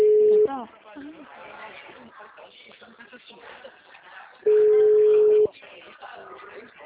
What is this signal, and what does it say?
Ringback tone (Europe, 425 Hz), with speech.